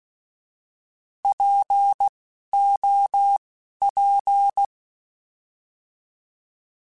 Morse code, heard as a keyed tone, about 16 wpm, spaced as for 11 wpm, 800 Hz. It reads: POP
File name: Alphabetical / Alphabetical mp3s / Ps @10